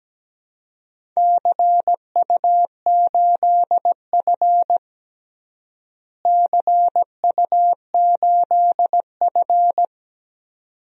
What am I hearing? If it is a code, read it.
Morse code, heard as a keyed tone, about 17 wpm, 700 Hz. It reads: CU8F CU8F
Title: DXCC-100 Call Signs - 4 Characters - Repeated 2x - 17wpm